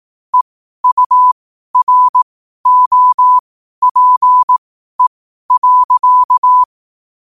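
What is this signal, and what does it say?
Morse code, heard as a keyed tone, about 18 wpm, spaced as for 12 wpm, 1000 Hz. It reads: EUROPE.
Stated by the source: Morse code quote practice at 12 words per minute.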